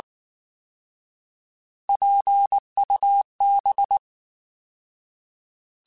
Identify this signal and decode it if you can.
Morse code, heard as a keyed tone, about 19 wpm, 800 Hz. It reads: PUB
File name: Alphabetical / Alphabetical mp3s / Ps @18